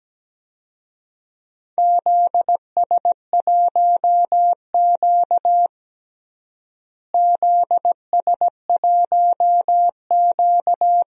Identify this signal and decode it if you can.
Morse code, heard as a keyed tone, about 17 wpm, 700 Hz. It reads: ZS1Q ZS1Q
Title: DXCC-100 Call Signs - 4 Characters - Repeated 3x - 17wpm